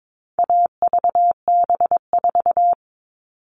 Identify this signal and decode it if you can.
Morse code, heard as a keyed tone, about 22 wpm, 700 Hz. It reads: AVB4
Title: All Cut Numbers - 4-Digit 22wpm